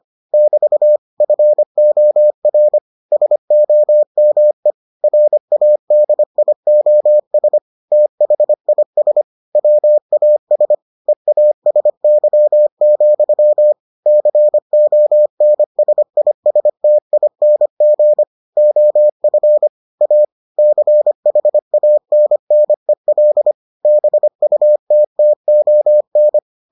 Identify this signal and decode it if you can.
Morse code, heard as a keyed tone, about 25 wpm, 600 Hz. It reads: <BT> FOR SOME RADIOS THIS WAS EASY, CONSISTING OF A CHANNEL BUTTON